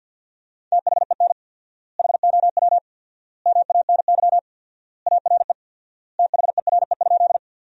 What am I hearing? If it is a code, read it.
Morse code, heard as a keyed tone, about 50 wpm, 700 Hz. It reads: THEN HOW MANY ARE THERE?
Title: 4x Word-Spacing - Sentences from Top 100 Words - 50wpm